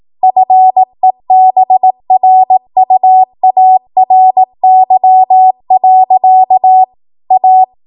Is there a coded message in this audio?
Yes: Morse code from 0.2 to 7.6 s, heard as a keyed tone, about 18 wpm, 750 Hz. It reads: FEBRUARY. A